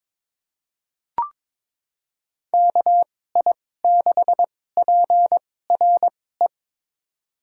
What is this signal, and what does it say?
Morse code, heard as a keyed tone, about 22 wpm, spaced as for 16 wpm, 700 Hz. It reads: KI6PRE
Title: USA Call Signs - 6 Characters - 2x Character Spacing - 22wpm